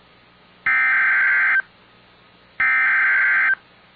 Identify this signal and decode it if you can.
Emergency Alert System (SAME) tones (header bursts), heard as harsh buzzing bursts. It is heard from 0.7 to 3.5 s.